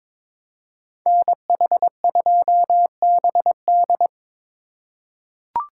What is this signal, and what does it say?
Morse code, heard as a keyed tone, about 22 wpm, 700 Hz. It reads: NH2BD
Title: USA Call Signs - 5 Characters - 2x Character Spacing - 22wpm